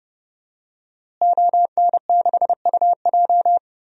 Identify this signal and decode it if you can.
Morse code, heard as a keyed tone, about 30 wpm, 700 Hz. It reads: ON6UJ
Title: DXCC-100 Call Signs - 5 Characters - Repeated 3x - 30wpm